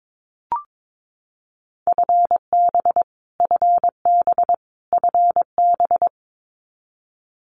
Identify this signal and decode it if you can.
Morse code, heard as a keyed tone, about 22 wpm, 700 Hz. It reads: FB FB FB
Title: Abbreviations Repeated 3x - 22wpm